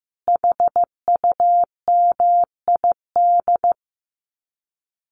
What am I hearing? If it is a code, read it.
Morse code, heard as a keyed tone, about 15 wpm, 700 Hz. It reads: HUMID
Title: Common QSO Phrases Encore 15wpm